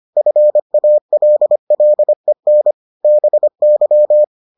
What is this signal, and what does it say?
Morse code, heard as a keyed tone, about 25 wpm, 600 Hz. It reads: FALLEN BY